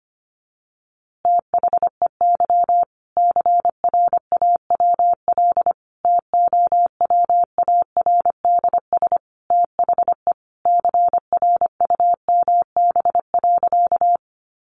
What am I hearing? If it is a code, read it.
Morse code, heard as a keyed tone, about 25 wpm, 700 Hz. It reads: THEY CRAWL TOWARDS THE CRUMB.
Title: Easy Phrases 25wpm